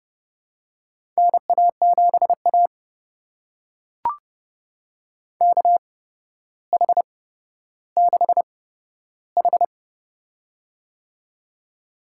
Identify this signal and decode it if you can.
Morse code, heard as a keyed tone, about 30 wpm, spaced as for 13 wpm, 700 Hz. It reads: NA7A KH6H
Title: USA Call Signs - 4 Characters - 8x Character Spacing - 30wpm